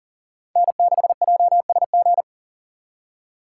Morse code, heard as a keyed tone, about 40 wpm, 700 Hz. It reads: N6JSG
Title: USA Call Signs - 2x Character Spacing - 40wpm